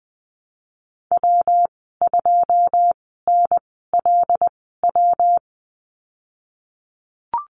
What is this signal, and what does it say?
Morse code, heard as a keyed tone, about 20 wpm, spaced as for 15 wpm, 700 Hz. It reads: W2NLW
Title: 128 Lesson 21g - Number 6 - Call Signs 20wpm